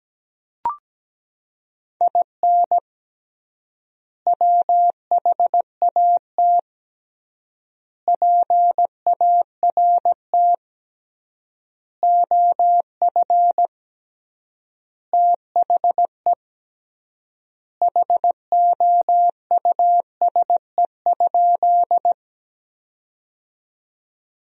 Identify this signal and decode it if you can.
Morse code, heard as a keyed tone, about 17 wpm, 700 Hz. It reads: IN WHAT PART OF THE HOUSE?
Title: 3x Word-Spacing - Sentences from Top 200 Words - 17wpm